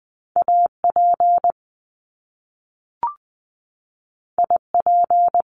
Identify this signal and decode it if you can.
Morse code, heard as a keyed tone, about 20 wpm, 700 Hz. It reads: AP IP
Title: Top Two Letter Combinations in Words 20wpm